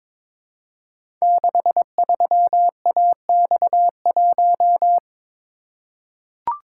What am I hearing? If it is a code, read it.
Morse code, heard as a keyed tone, about 23 wpm, 700 Hz. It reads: 63AX1
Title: Code Groups - 5 Letter:Numbers - 22wpm